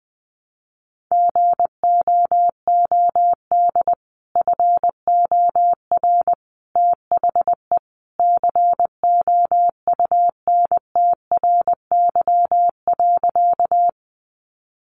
Morse code, heard as a keyed tone, about 20 wpm, 700 Hz. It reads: GOOD FOR THE COUNTRY.
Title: Sentences from Top 300 Words 20wpm